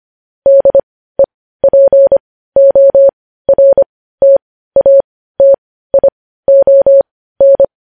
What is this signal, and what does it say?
Morse code, heard as a keyed tone, about 25 wpm, spaced as for 15 wpm, 550 Hz. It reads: DEPORTATION
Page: news headlines in morse code at 15 wpm